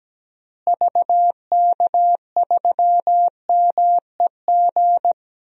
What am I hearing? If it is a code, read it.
Morse code, heard as a keyed tone, about 17 wpm, 700 Hz. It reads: VK3MEG
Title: DXCC-100 Call Signs - 6 Characters - Repeated 3x - 17wpm